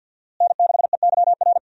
Morse code, heard as a keyed tone, about 50 wpm, 700 Hz. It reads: N6EYR
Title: USA Call Signs - 5 Characters - Repeated 3x - 50wpm